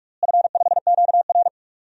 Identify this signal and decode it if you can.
Morse code, heard as a keyed tone, about 41 wpm, 700 Hz. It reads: F5QR
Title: DXCC-100 Call Signs - 4 Characters - Repeated 2x - 45wpm